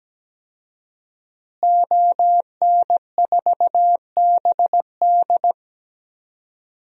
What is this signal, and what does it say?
Morse code, heard as a keyed tone, about 17 wpm, 700 Hz. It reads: ON4BD
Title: DXCC-100 Call Signs - 5 Characters - Repeated 2x - 17wpm